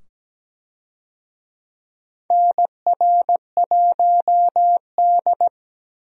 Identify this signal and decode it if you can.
Morse code, heard as a keyed tone, about 17 wpm, 700 Hz. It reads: NR1D
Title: USA Call Signs - 4 Characters - 17wpm